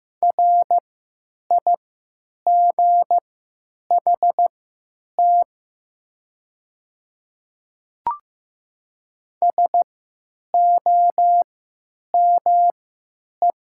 Morse code, heard as a keyed tone, about 15 wpm, spaced as for 9 wpm, 700 Hz. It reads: RIGHT SOME
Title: 3x Character Spacing - Top 300 Words - 15wpm